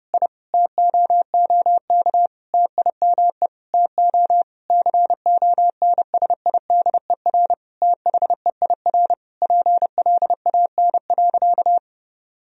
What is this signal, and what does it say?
Morse code, heard as a keyed tone, about 30 wpm, 700 Hz. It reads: I TOOK TIME TO CONSIDER THEIR PLAN.
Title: Sentences from Top 1000 Words - Review Part 4 - 30wpm